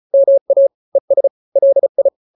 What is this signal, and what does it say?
Morse code, heard as a keyed tone, about 34 wpm, 550 Hz. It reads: MA ES LI
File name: n-grams / 2-grams 35 30 WPM